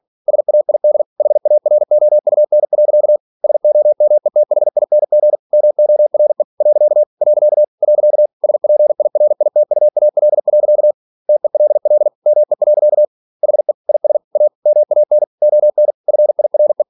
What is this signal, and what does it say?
Morse code, heard as a keyed tone, about 46 wpm, 600 Hz. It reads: SAID HAROUN. SOMETHING MORE . . SPIRITUAL. TELL ME. HE IS A MAN ON FIRE
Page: Never 47wpm